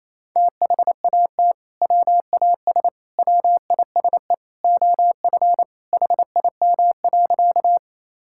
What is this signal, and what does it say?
Morse code, heard as a keyed tone, about 28 wpm, 700 Hz. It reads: THAT WAS WISE OF HIM.
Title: Sentences from Top 1600 Words 28wpm